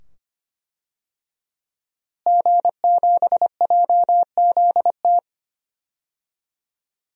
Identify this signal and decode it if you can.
Morse code, heard as a keyed tone, about 25 wpm, 700 Hz. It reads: G7JZT
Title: DXCC-100 Call Signs - 5 Characters - Repeated 2x - 25wpm